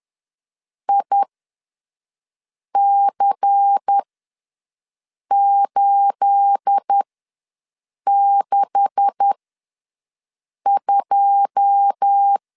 Morse code, heard as a keyed tone, about 10 wpm, spaced as for 5 wpm, 800 Hz. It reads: IC862
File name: LEZIONI / 9 286 ws 10wpm 4032